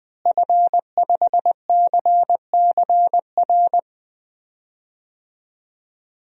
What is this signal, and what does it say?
Morse code, heard as a keyed tone, about 21 wpm, 700 Hz. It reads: F5CCR
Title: DXCC-100 Call Signs - 5 Characters - Repeated 3x - 20wpm